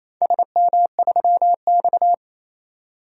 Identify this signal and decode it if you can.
Morse code, heard as a keyed tone, about 28 wpm, 700 Hz. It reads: SM3X
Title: DXCC-100 Call Signs - 4 Characters - 28wpm